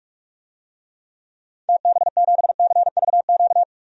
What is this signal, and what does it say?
Morse code, heard as a keyed tone, about 42 wpm, 700 Hz. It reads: TB7CVQ
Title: DXCC-100 Call Signs - 6 Characters - 45wpm